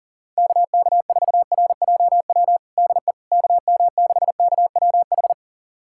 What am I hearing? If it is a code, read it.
Morse code, heard as a keyed tone, about 40 wpm, 700 Hz. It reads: KK4RJW DE KM6KWH
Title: USA Callsign DE Callsign 40wpm